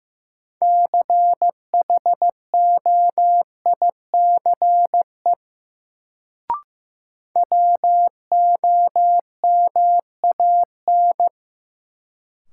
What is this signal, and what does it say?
Morse code, heard as a keyed tone, about 15 wpm, 700 Hz. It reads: CHOICE WOMAN
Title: Sets of 5 Words - New Word Drill - 15wpm